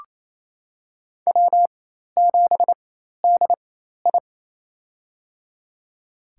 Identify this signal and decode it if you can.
Morse code, heard as a keyed tone, about 27 wpm, spaced as for 13 wpm, 700 Hz. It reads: W7DI
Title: USA Call Signs - 4 Characters - 4x Character Spacing - 28wpm